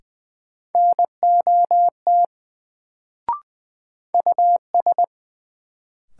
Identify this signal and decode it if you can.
Morse code, heard as a keyed tone, about 20 wpm, 700 Hz. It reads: NOT US
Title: Sets of 2 Words - New Word Drill - 20wpm